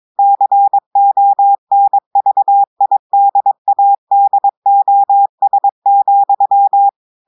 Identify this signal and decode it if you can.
Morse code, heard as a keyed tone, about 22 wpm, 800 Hz. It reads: CONVIDADOS,
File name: audios / portugais / portugais texte en clair 1 vitesse 22 mots minutes